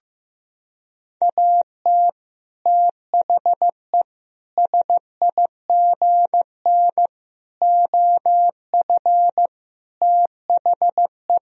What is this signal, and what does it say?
Morse code, heard as a keyed tone, about 15 wpm, 700 Hz. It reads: AT THE SIGN OF THE
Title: Sentences from Top 1000 Words - Review Part 1 - 15wpm